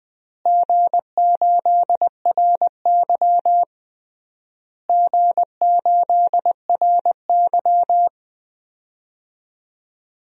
Morse code, heard as a keyed tone, about 21 wpm, 700 Hz. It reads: G8RY G8RY
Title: DXCC-100 Call Signs - 4 Characters - Repeated 3x - 20wpm